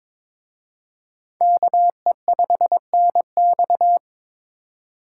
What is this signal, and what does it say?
Morse code, heard as a keyed tone, about 26 wpm, 700 Hz. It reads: KE5NX
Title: Speed Racing - USA 2x2 Call Signs 22wpm